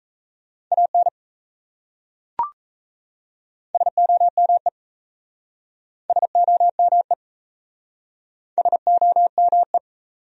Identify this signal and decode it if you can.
Morse code, heard as a keyed tone, about 36 wpm, 700 Hz. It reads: AN SOME SOME SOME